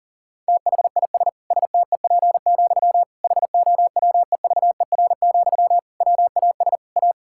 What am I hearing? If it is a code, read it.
Morse code, heard as a keyed tone, about 40 wpm, 700 Hz. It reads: THIS STEP, HOWEVER, WAS A
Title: Sentences from Top 1000 Words - Review Part 6 - 40wpm